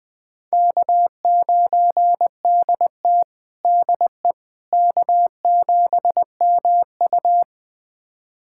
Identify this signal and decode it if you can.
Morse code, heard as a keyed tone, about 20 wpm, 700 Hz. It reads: K9DT DE K7MU
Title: USA Callsign DE Callsign - 4 Characters - 20wpm